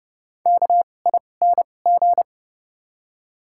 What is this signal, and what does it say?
Morse code, heard as a keyed tone, about 31 wpm, spaced as for 22 wpm, 700 Hz. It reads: KING
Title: 2x Character Spacing - Top 500 Words - 30wpm